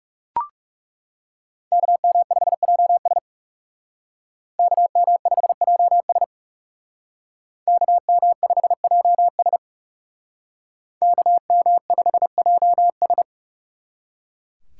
Morse code, heard as a keyed tone, about 36 wpm, 700 Hz. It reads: KM5JS KM5JS KM5JS KM5JS
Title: Speed Racing - USA Call Signs 30wpm